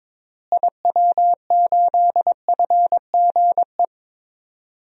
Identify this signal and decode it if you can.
Morse code, heard as a keyed tone, about 22 wpm, 700 Hz. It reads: IW8FGE
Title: DXCC-100 Call Signs - 6 Characters - Repeated 2x - 22wpm